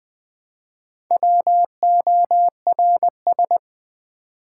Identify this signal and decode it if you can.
Morse code, heard as a keyed tone, about 20 wpm, 700 Hz. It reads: WORS
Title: Build A Word - 1 Letter At A Time - 3 to 5-Letter Words - 20wpm